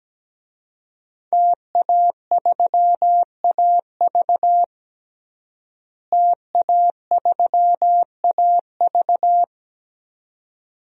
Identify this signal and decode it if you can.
Morse code, heard as a keyed tone, about 17 wpm, 700 Hz. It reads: TA3AV TA3AV
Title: DXCC-100 Call Signs - 5 Characters - Repeated 2x - 17wpm